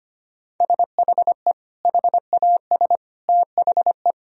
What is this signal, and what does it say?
Morse code, heard as a keyed tone, about 25 wpm, 700 Hz. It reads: SHE HAS THE
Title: Sentences from Top 1000 Words - Review Part 3 - 25wpm